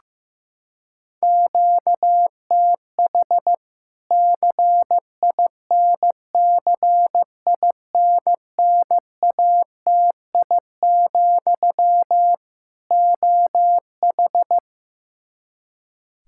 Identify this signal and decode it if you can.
Morse code, heard as a keyed tone, about 15 wpm, 700 Hz. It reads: QTH CINCINNATI, OH
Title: Practice QSO - Exchange 1 - Beginning - 15wpm